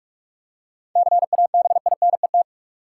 Morse code, heard as a keyed tone, about 46 wpm, 700 Hz. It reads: CABINET
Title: Seven Letter Words - 45wpm